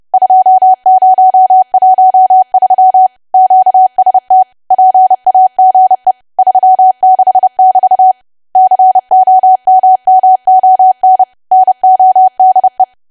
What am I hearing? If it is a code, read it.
Morse code, heard as a keyed tone, about 30 wpm, 750 Hz. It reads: 2013 QST PAGE 36 <BT> COMMON NODE
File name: cwa / W1AW-Level3 / 30WPM